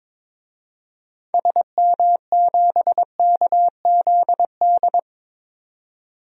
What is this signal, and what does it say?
Morse code, heard as a keyed tone, about 21 wpm, 700 Hz. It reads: SM7KZD
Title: DXCC-100 Call Signs - 6 Characters - 22wpm